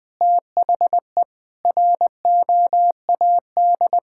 Morse code, heard as a keyed tone, about 19 wpm, 700 Hz. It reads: THE ROAD
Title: Sentences with 3-Word Phrases - Spoken Beginning - 20wpm